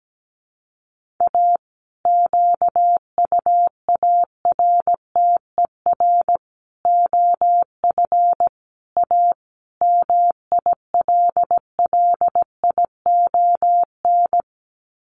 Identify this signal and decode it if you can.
Morse code, heard as a keyed tone, about 17 wpm, 700 Hz. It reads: A QUARTER OF A MILLION
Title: Sets of 5 Words Encore2 17wpm